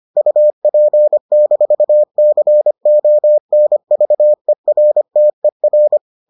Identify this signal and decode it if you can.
Morse code, heard as a keyed tone, about 25 wpm, 600 Hz. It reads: UP-CONVERTER